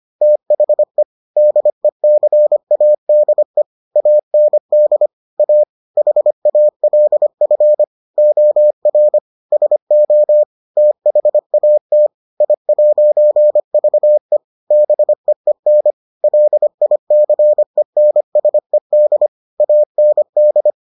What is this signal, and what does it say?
Morse code, heard as a keyed tone, about 25 wpm, 600 Hz. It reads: THE DECADE AND A HALF OR SO THAT I'VE BEEN LICENSED AND